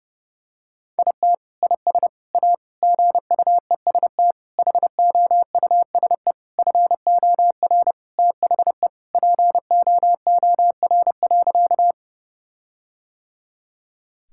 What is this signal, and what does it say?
Morse code, heard as a keyed tone, about 28 wpm, 700 Hz. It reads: IT IS A GUEST HOUSE FOR THE POOR.
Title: Sentences from Top 1600 Words 30wpm